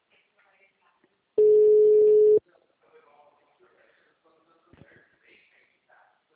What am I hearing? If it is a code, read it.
Ringback tone (Europe, 425 Hz), with speech.